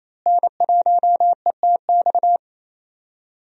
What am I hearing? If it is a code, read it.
Morse code, heard as a keyed tone, about 28 wpm, 700 Hz. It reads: N1ETX
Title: USA Call Signs - 5 Characters - Repeated 3x - 28wpm